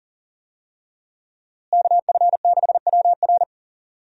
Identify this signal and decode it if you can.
Morse code, heard as a keyed tone, about 40 wpm, 700 Hz. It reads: KF6WR
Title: USA Call Signs - 5 Characters - Repeated 2x - 40wpm